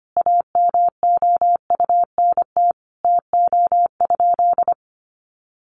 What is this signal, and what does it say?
Morse code, heard as a keyed tone, about 25 wpm, 700 Hz. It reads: AMOUNT TO?
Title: Sentences from Top 700 Words 25wpm